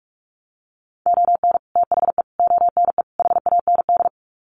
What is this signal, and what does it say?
Morse code, heard as a keyed tone, about 44 wpm, 700 Hz. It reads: ON THE ONE HAND
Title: Sets of 4 Words 45wpm